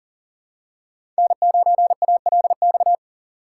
Morse code, heard as a keyed tone, about 40 wpm, 700 Hz. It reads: N9ALX
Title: USA Call Signs - 5 Characters - 6x Character Spacing - 40wpm